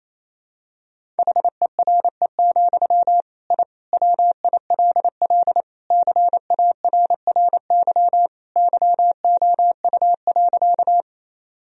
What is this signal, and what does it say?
Morse code, heard as a keyed tone, about 28 wpm, 700 Hz. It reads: HERE, I WILL CARRY YOU.
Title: Sentences from Top 1000 Words - Review Part 1 - 28wpm